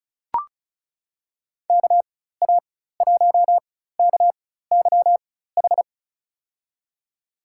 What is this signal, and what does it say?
Morse code, heard as a keyed tone, about 36 wpm, spaced as for 16 wpm, 700 Hz. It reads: KA1KYH
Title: USA Call Signs - 6 Characters - 4x Character Spacing - 35wpm